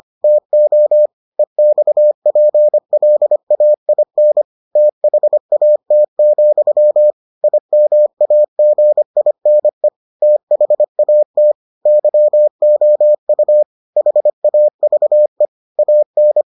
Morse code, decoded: TO EXPLAIN THAT, IMAGINE THAT YOU HAVE AN

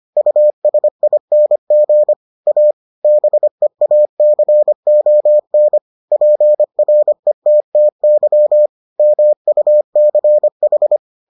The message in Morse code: USING A BEACON PRETTY MUCH

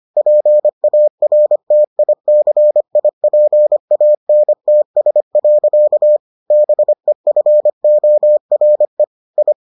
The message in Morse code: PARTICIPANTS. BEFORE I

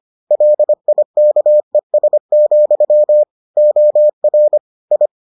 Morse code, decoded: LIKES, OR I

